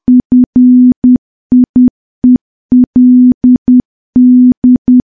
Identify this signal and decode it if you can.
Morse code, heard as a keyed tone, about 10 wpm, 250 Hz. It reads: FIELD
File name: morse / morse-training-10WPM